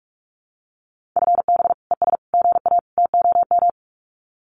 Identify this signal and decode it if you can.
Morse code, heard as a keyed tone, about 44 wpm, 700 Hz. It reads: FB ES GA TOM
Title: Common QSO Phrases Encore 45wpm